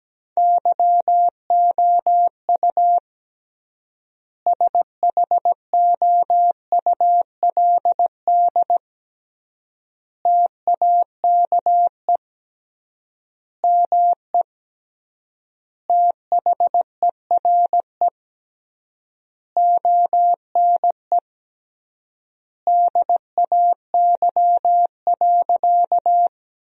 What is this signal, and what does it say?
Morse code, heard as a keyed tone, about 17 wpm, 700 Hz. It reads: YOU SHOULD TAKE ME THERE ONE DAY.
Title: 3x Word-Spacing - Sentences from Top 300 Words - 17wpm